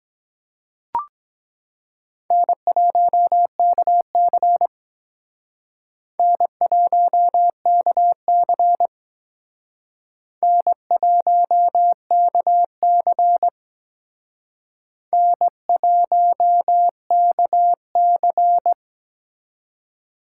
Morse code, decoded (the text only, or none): N1KC N1KC N1KC N1KC